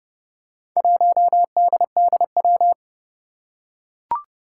Morse code, heard as a keyed tone, about 30 wpm, 700 Hz. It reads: 1DDW
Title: Four Challenging Letter-Number Combinations 30wpm